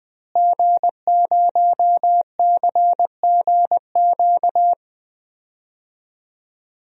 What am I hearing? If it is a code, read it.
Morse code, heard as a keyed tone, about 20 wpm, 700 Hz. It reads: G0CGQ
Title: DXCC-100 Call Signs - 5 Characters - Repeated 3x - 20wpm